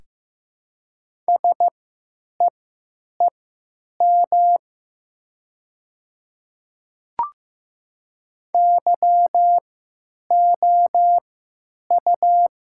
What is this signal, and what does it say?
Morse code, heard as a keyed tone, about 15 wpm, spaced as for 9 wpm, 700 Hz. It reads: SEEM YOU
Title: 3x Character Spacing - Top 300 Words - 15wpm